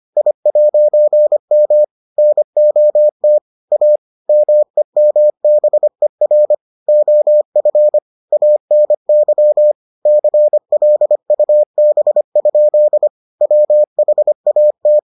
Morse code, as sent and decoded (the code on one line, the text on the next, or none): .. .----. -- / -. --- - / .- / -- . -- -... .-. / --- ..-. / .- -. -.-- / -.-. .-.. ..- -... ..--.. / .-- .... .- -
I'M NOT A MEMBER OF ANY CLUB? WHAT